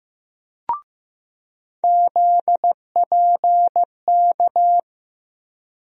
Morse code, heard as a keyed tone, about 15 wpm, 700 Hz. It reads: ZPK
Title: USA Call Sign Postfixes - 3 Letters 15wpm